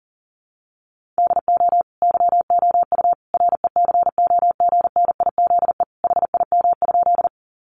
Morse code, heard as a keyed tone, about 40 wpm, 700 Hz. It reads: DO YOU RECOGNIZE HIM?
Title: Sentences from Top 1400 Words 40wpm